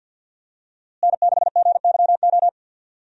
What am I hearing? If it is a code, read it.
Morse code, heard as a keyed tone, about 51 wpm, 700 Hz. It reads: N6GYO
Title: USA Call Signs - 5 Characters - Repeated 3x - 50wpm